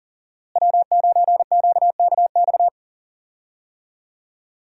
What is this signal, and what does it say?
Morse code, heard as a keyed tone, about 40 wpm, 700 Hz. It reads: W9QKX
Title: USA Call Signs - 5 Characters - Repeated 2x - 40wpm